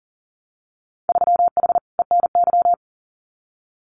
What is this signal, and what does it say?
Morse code, heard as a keyed tone, about 40 wpm, 700 Hz. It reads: 3H ENY
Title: Winter Field Day Exchanges Encore 40wpm